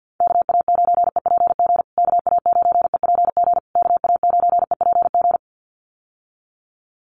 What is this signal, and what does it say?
Morse code, heard as a keyed tone, about 50 wpm, 700 Hz. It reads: KA9EPG KA9EPG KA9EPG
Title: USA Call Signs Repeated 3x 50wpm